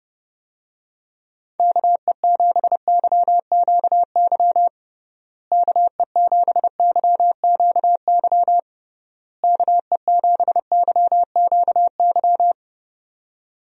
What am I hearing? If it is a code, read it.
Morse code, heard as a keyed tone, about 32 wpm, 700 Hz. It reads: KE7YQY KE7YQY KE7YQY
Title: USA Call Signs - 6 Characters - Repeated 3x - 30wpm